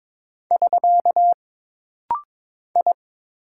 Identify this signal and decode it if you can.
Morse code, heard as a keyed tone, about 22 wpm, 700 Hz. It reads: <SK> I
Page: This rapid-fire timing is geared to push your ICR (Instant Character Recognition) skills.